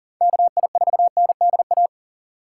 Morse code, heard as a keyed tone, about 40 wpm, 700 Hz. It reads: KI4NDA
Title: USA Call Signs - 6 Characters - 2x Character Spacing - 40wpm